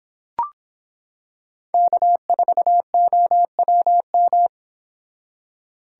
Morse code, heard as a keyed tone, about 25 wpm, 700 Hz. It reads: K4OWM